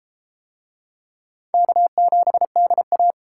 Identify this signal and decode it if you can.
Morse code, heard as a keyed tone, about 31 wpm, 700 Hz. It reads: K7DA